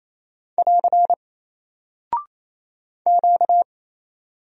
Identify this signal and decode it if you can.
Morse code, heard as a keyed tone, about 28 wpm, 700 Hz. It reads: <AR> Q
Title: Single Character (Letter:Number:Punctuation:Prosign) Alt 28wpm